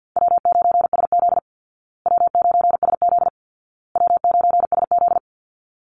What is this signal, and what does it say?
Morse code, heard as a keyed tone, about 50 wpm, 700 Hz. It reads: W9SZ W9SZ W9SZ
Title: Easy Call Signs Repeated 3x - 50wpm